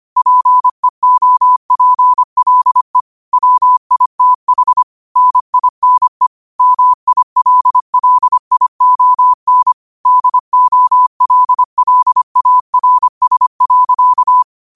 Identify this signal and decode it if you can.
Morse code, heard as a keyed tone, about 25 wpm, 1000 Hz. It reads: PEOPLE WITH NINE MILLION DOLLARS.
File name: qotdfiles / 25wpm